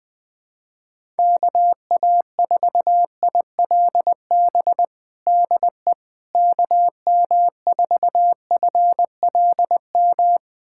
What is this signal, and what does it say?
Morse code, heard as a keyed tone, about 20 wpm, 700 Hz. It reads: KA4ILB DE KM4FLM
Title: USA Callsign DE Callsign 20wpm